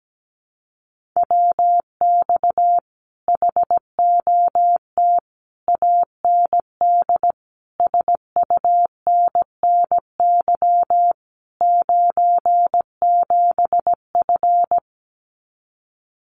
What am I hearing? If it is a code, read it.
Morse code, heard as a keyed tone, about 17 wpm, 700 Hz. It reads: WX HOT AND SUNNY 97F
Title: Common QSO Phrases Encore 17wpm